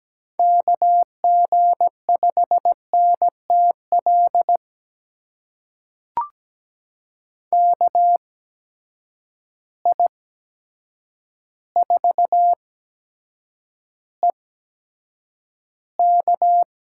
Morse code, heard as a keyed tone, about 17 wpm, spaced as for 8 wpm, 700 Hz. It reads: KG5NTL KI4EK
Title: USA Call Signs - 6 Characters - 8x Character Spacing - 17wpm